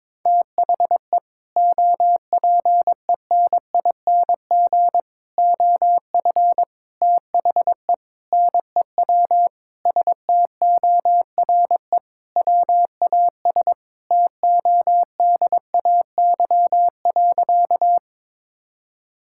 Morse code, heard as a keyed tone, about 22 wpm, 700 Hz. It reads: THE OPENING OF THE NEW STORE WAS TODAY.
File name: practice / 22wpm / Sentences with 3-Word Phrases - 22wpm